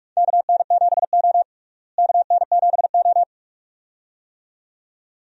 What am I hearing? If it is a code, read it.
Morse code, heard as a keyed tone, about 41 wpm, 700 Hz. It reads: KN7O KN7O
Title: USA Call Signs - 4 Characters - Repeated 2x - 45wpm